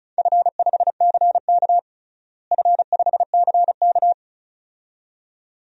Morse code, heard as a keyed tone, about 36 wpm, 700 Hz. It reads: F5CK F5CK
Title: DXCC-100 Call Signs - 4 Characters - Repeated 2x - 35wpm